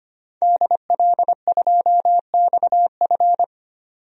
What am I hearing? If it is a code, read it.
Morse code, heard as a keyed tone, about 25 wpm, 700 Hz. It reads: DL2XF
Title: DXCC-100 Call Signs - 5 Characters - 25wpm